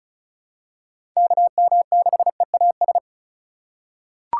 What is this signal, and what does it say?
Morse code, heard as a keyed tone, about 36 wpm, 700 Hz. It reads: KM6EAS
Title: USA Call Signs - 6 Characters - 3x Character Spacing - 35wpm